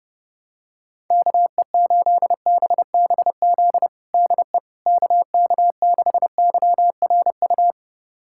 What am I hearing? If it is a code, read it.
Morse code, heard as a keyed tone, about 31 wpm, 700 Hz. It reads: KE8BBZ DE KK6YRU
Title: USA Callsign DE Callsign - 6 Characters - 30wpm